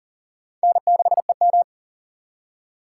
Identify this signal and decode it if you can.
Morse code, heard as a keyed tone, about 52 wpm, 700 Hz. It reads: N6EM